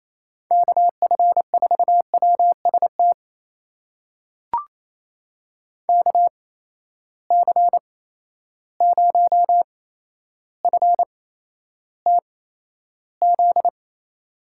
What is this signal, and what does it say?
Morse code, heard as a keyed tone, about 28 wpm, spaced as for 12 wpm, 700 Hz. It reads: KF4WST KC0FTZ
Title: USA Call Signs - 6 Characters - 8x Character Spacing - 28wpm